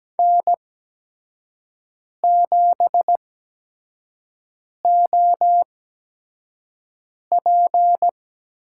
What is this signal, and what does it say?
Morse code, heard as a keyed tone, about 17 wpm, spaced as for 5 wpm, 700 Hz. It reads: N7OP
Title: USA Call Signs - 4 Characters - 8x Character Spacing - 17wpm